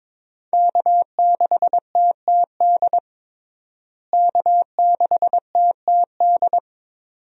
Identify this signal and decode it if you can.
Morse code, heard as a keyed tone, about 22 wpm, 700 Hz. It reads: K6TTD K6TTD
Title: USA Call Signs - 5 Characters - Repeated 2x - 22wpm